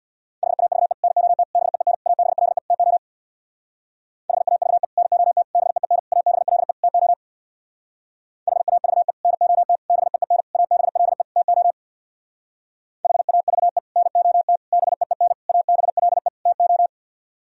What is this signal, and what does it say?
Morse code, heard as a keyed tone, about 62 wpm, 700 Hz. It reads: HAVE NOT BEEN ABLE TO HAVE NOT BEEN ABLE TO HAVE NOT BEEN ABLE TO HAVE NOT BEEN ABLE TO
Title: Speed-Racing - Sets of 5 Words 50wpm